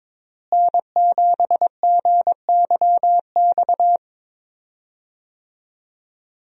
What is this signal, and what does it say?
Morse code, heard as a keyed tone, about 21 wpm, 700 Hz. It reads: N7GYX
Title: USA Call Signs - 5 Characters - Repeated 3x - 22wpm